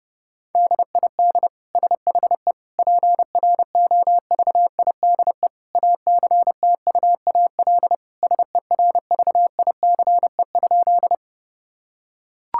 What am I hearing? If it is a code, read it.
Morse code, heard as a keyed tone, about 30 wpm, 700 Hz. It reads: DID SHE PROVIDE ACTUAL SERVICE?
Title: Sentences from Top 1000 Words - Review Part 5 - 30wpm